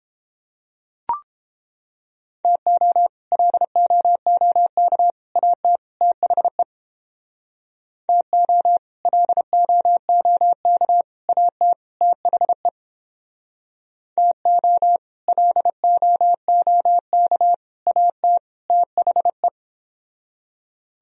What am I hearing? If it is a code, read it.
Morse code, heard as a keyed tone, about 30 wpm, 700 Hz. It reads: TO LOOK AT THE TO LOOK AT THE TO LOOK AT THE